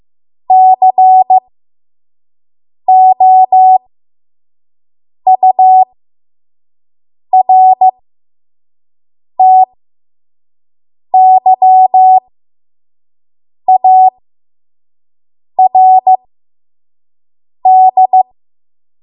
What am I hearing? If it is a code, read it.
Morse code, heard as a keyed tone, about 15 wpm, spaced as for 5 wpm, 750 Hz. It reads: COURTYARD
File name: Morse / 05WPM